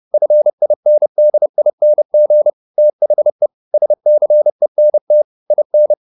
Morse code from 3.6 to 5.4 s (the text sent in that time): SCENT